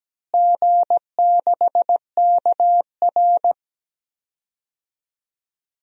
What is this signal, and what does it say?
Morse code, heard as a keyed tone, about 17 wpm, 700 Hz. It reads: G6KR
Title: DXCC-100 Call Signs - 4 Characters - Repeated 2x - 17wpm